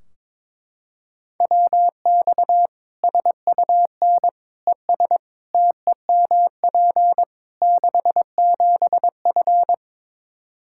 Morse code, heard as a keyed tone, about 22 wpm, 700 Hz. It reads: WX SUN ES TEMP 67F
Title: Practice QSO - Exchange 2 - Info - 22wpm